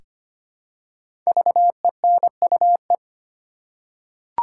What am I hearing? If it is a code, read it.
Morse code, heard as a keyed tone, about 25 wpm, 700 Hz. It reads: VENUE
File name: 5 Letter Words - First 3 Letters Given - 25wpm